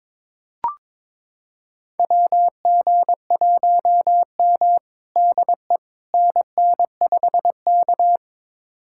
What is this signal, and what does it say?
Morse code, heard as a keyed tone, about 22 wpm, 700 Hz. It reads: WG1M DE NN5K
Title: USA Callsign DE Callsign - 4 Characters - 22wpm